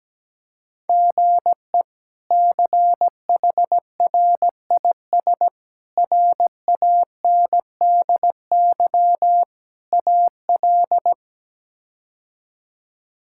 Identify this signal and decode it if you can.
Morse code, heard as a keyed tone, about 17 wpm, 700 Hz. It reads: GE CHRIS RANDY AL
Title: K1USN SST QSO Practice - 17wpm